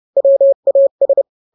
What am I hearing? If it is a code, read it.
Morse code, heard as a keyed tone, about 30 wpm, 550 Hz. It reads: WAS